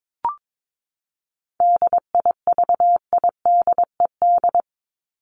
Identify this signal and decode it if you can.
Morse code, heard as a keyed tone, about 22 wpm, 700 Hz. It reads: DIVIDED
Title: ED Suffix - Top 100 Words with Base Words 22wpm